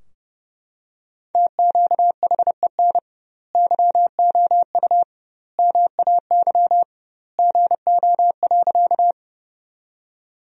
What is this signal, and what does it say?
Morse code, heard as a keyed tone, about 30 wpm, 700 Hz. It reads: TQHEN YOU MAY GO.
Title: 2x Word-Spacing - Sentences from Top 100 Words - 30wpm